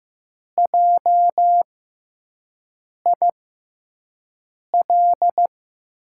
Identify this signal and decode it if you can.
Morse code, heard as a keyed tone, about 16 wpm, spaced as for 5 wpm, 700 Hz. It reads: JIL
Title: USA Call Signs - 6 Characters - 6x Character Spacing - 15wpm